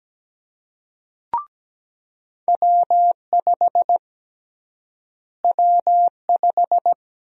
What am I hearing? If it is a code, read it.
Morse code, heard as a keyed tone, about 17 wpm, 700 Hz. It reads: W5 W5
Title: USA Call Sign Prefixes - Repeated 2x - 17wpm